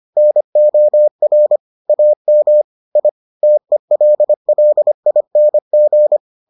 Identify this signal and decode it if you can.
Morse code, heard as a keyed tone, about 25 wpm, 600 Hz. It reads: NOR AM I TELLING